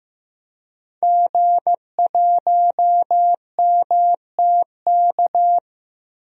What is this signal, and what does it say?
Morse code, heard as a keyed tone, about 15 wpm, 700 Hz. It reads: G1MTK
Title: DXCC-100 Call Signs - 5 Characters - Repeated 2x - 15wpm